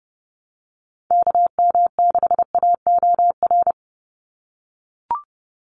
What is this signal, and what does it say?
Morse code, heard as a keyed tone, about 32 wpm, 700 Hz. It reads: KM6AOR
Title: USA Call Signs 30wpm